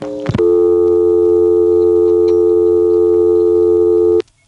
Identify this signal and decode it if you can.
Dial tone (North America), 350 + 440 Hz.